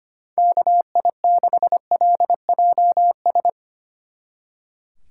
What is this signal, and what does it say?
Morse code, heard as a keyed tone, about 26 wpm, 700 Hz. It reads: KI6LJS
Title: Speed Racing - USA 2x3 Call Signs 25wpm